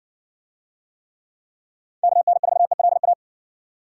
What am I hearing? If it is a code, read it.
Morse code, heard as a keyed tone, about 55 wpm, 700 Hz. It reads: KN4ELA